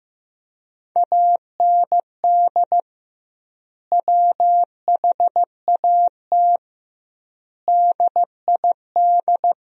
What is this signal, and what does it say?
Morse code, heard as a keyed tone, about 15 wpm, 700 Hz. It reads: AND WHAT DID
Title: 2x Word-Spacing - Sentences from Top 100 Words - 15wpm